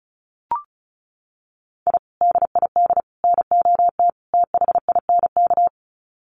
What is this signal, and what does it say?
Morse code, heard as a keyed tone, about 34 wpm, 700 Hz. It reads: I DID NOT THINK
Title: Sets of 4 Words 35wpm